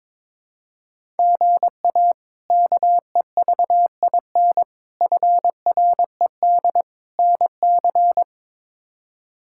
Morse code, heard as a keyed tone, about 22 wpm, 700 Hz. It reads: GA KEVIN FRED NC
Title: K1USN SST QSO Practice - 22wpm